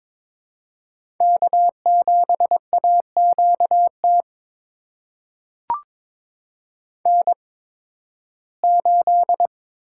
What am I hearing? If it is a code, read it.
Morse code, heard as a keyed tone, about 22 wpm, spaced as for 14 wpm, 700 Hz. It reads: K7AQT N8
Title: USA Call Signs - 5 Characters - 8x Character Spacing - 22wpm